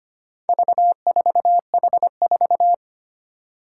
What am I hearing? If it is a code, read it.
Morse code, heard as a keyed tone, about 25 wpm, 700 Hz. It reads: V4H4
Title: Four Challenging Letter-Number Combinations 25wpm